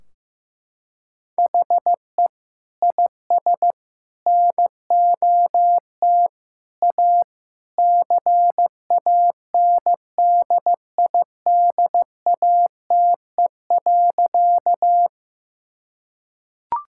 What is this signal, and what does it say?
Morse code, heard as a keyed tone, about 15 wpm, 700 Hz. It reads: HE IS NOT A CANDIDATE.
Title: Sentences from Top 1600 Words 15wpm